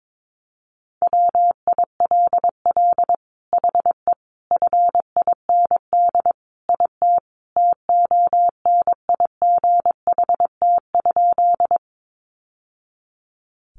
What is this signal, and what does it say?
Morse code, heard as a keyed tone, about 22 wpm, 700 Hz. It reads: WILL HE FIND IT TONIGHT?
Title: Sentences from Top 600 Words 22wpm